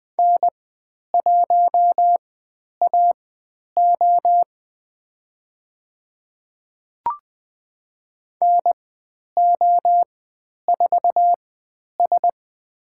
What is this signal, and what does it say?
Morse code, heard as a keyed tone, about 20 wpm, spaced as for 10 wpm, 700 Hz. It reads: N1AO NO4S
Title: 102 Lesson 17g - Letter u - Call Signs 20wpm